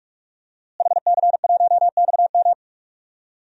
Morse code, heard as a keyed tone, about 46 wpm, 700 Hz. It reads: HC1XM
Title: DXCC-100 Call Signs - 5 Characters - Repeated 3x - 45wpm